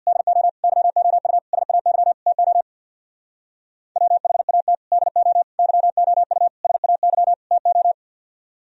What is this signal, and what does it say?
Morse code, heard as a keyed tone, about 61 wpm, 700 Hz. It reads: DO YOU SAY TO WHAT DO YOU SAY TO
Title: Speed-Racing - Sets of 5 Words 50wpm